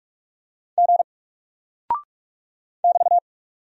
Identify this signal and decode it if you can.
Morse code, heard as a keyed tone, about 44 wpm, 700 Hz. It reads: G <BT>
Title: Single Character (Letter:Number:Punctuation:Prosign) Alt 45wpm